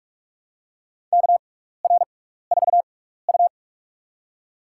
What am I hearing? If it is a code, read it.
Morse code, heard as a keyed tone, about 42 wpm, spaced as for 15 wpm, 700 Hz. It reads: KR4U
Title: USA Call Signs - 4 Characters - 6x Character Spacing - 45wpm